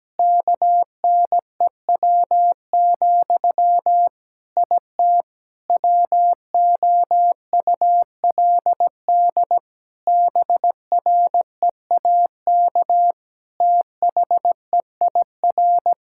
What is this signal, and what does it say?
Morse code, heard as a keyed tone, about 17 wpm, 700 Hz. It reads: KNEW, IT WOULD BREAK THEIR
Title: Sentences from Top 1000 Words - Review Part 6 - 17wpm